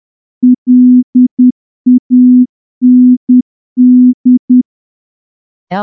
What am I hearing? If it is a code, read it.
Morse code, heard as a keyed tone, about 10 wpm, 250 Hz. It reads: LAND
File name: morse-training-10WPM